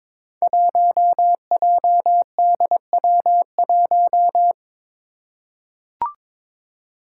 Morse code, heard as a keyed tone, about 22 wpm, 700 Hz. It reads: 1JDW1
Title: Five Challenging Letter-Number Combinations 22wpm